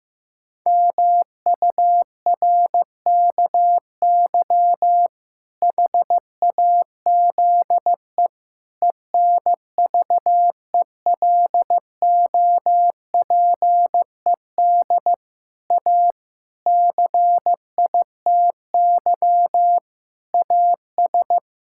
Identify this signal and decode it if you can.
Morse code, heard as a keyed tone, about 15 wpm, 700 Hz. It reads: MURKY HAZE ENVELOPED A CITY AS
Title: Pangram Challenge v2 - 15wpm